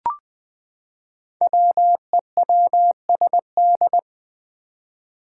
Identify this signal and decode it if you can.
Morse code, heard as a keyed tone, about 20 wpm, 700 Hz. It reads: WEWSD
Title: Five Challenging Letter-Number Combinations 20wpm